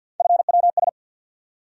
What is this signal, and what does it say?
Morse code, heard as a keyed tone, about 50 wpm, 700 Hz. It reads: FWS
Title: USA Call Sign Postfixes - 3 Letters 50wpm